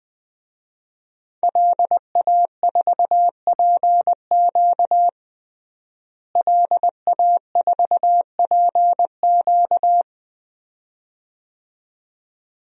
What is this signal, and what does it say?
Morse code, heard as a keyed tone, about 20 wpm, 700 Hz. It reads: LA4PQ LA4PQ
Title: DXCC-100 Call Signs - 5 Characters - Repeated 2x - 20wpm